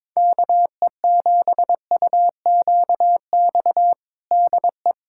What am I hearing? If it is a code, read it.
Morse code, heard as a keyed tone, about 22 wpm, 700 Hz. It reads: KE7UQX DE
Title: USA Callsign DE Callsign 22wpm